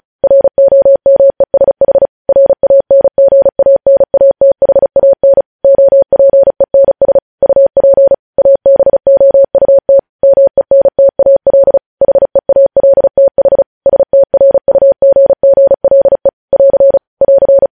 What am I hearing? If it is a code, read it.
Morse code, heard as a keyed tone, about 36 wpm, 550 Hz. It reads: ROMESH RANGANATHAN OPENS UP ABOUT MENTAL HEALTH STRUGGLE <AR> <AR>
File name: podcast / News headlines-1x-35x35